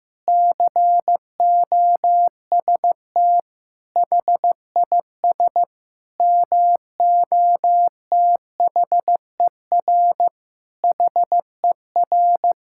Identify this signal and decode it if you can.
Morse code, heard as a keyed tone, about 15 wpm, 700 Hz. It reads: COST HIS MOTHER HER
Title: Sentences from Top 1000 Words - Review Part 5 - 15wpm